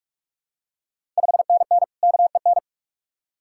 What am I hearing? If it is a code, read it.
Morse code, heard as a keyed tone, about 44 wpm, 690 Hz. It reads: 5NN KEN
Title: Ontario QSO Party Exchange 45wpm